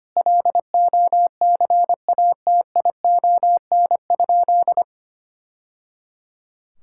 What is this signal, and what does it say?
Morse code, heard as a keyed tone, about 25 wpm, 700 Hz. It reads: LOCATION?
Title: Sentences from Top 1000 Words - Review Part 6 - 25wpm